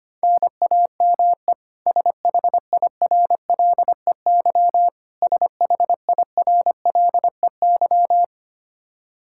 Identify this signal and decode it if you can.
Morse code, heard as a keyed tone, about 25 wpm, 700 Hz. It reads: NAME SHIRLEY SHIRLEY
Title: K1USN SST QSO Practice 25wpm